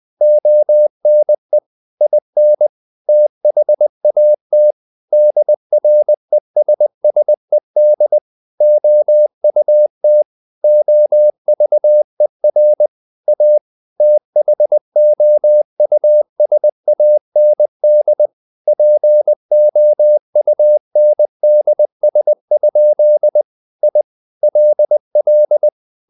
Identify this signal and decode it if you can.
Morse code, heard as a keyed tone, about 20 wpm, 600 Hz. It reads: ONE IN THAT DRESSED OUT OVER A THOUSAND POUNDS? I LL
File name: the old man and the sea / the old man and the sea part 1 20wpm